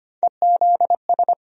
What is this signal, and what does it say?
Morse code, heard as a keyed tone, about 25 wpm, 700 Hz. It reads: EZS
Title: USA Call Sign Postfixes - 3 Letters 25wpm